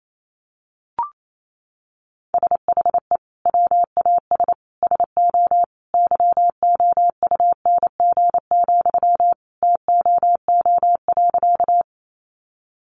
Morse code, heard as a keyed tone, about 28 wpm, 700 Hz. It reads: SHE WAS SO YOUNG, TOO.
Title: Sentences from Top 400 Words 28wpm